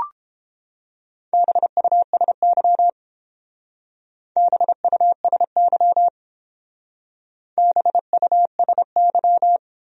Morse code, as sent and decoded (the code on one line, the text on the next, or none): -... ..- ... -.-- / -... ..- ... -.-- / -... ..- ... -.--
BUSY BUSY BUSY